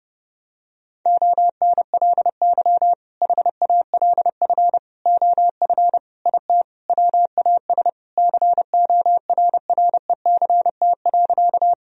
Morse code, heard as a keyed tone, about 32 wpm, 700 Hz. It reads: ONLY HALF OF IT WAS CORRECT.
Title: Sentences from Top 1000 Words - Review Part 2 - 30wpm